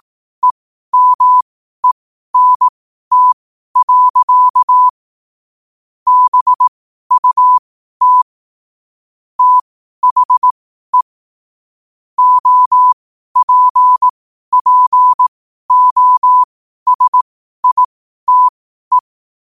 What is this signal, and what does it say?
Morse code, heard as a keyed tone, about 18 wpm, spaced as for 12 wpm, 1000 Hz. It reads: EMENT. BUT THE OPPOSITE
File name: qotdfiles / 12wpm